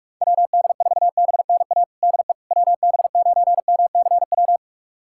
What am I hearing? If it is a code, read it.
Morse code, heard as a keyed tone, about 42 wpm, 690 Hz. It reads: WD4BNA DE WB9MCW
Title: USA Callsign DE Callsign - 6 Characters - 45wpm